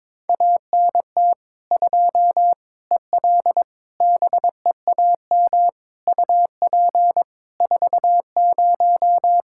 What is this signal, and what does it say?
Morse code, heard as a keyed tone, about 22 wpm, 700 Hz. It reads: ANT 2 EL BEAM UP 40
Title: Practice QSO - Exchange 2 - Info - 22wpm